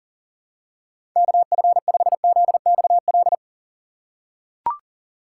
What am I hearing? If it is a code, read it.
Morse code, heard as a keyed tone, about 37 wpm, 700 Hz. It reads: KF5ZXL
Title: USA Call Signs - 2x Character Spacing - 40wpm